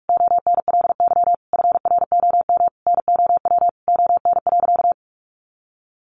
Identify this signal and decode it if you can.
Morse code, heard as a keyed tone, about 44 wpm, 700 Hz. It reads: ONLY FROM NOW ON.
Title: Easy Phrases 45wpm